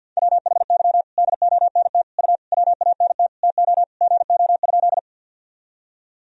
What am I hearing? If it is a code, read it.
Morse code, heard as a keyed tone, about 52 wpm, 700 Hz. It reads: WHY DONT U WANT TO GO?
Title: Sentences from Top 300 Words and Abbreviations 50wpm